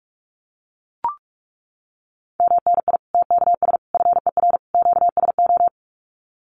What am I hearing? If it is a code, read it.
Morse code, heard as a keyed tone, about 44 wpm, 700 Hz. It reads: MNI TKS FER QSO
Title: Common QSO Phrases 45wpm